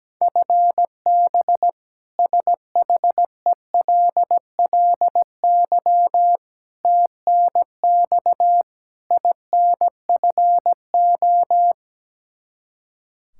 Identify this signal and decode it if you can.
Morse code, heard as a keyed tone, about 17 wpm, 700 Hz. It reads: FB SHELLY TNX INFO
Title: K1USN SST QSO Practice 17wpm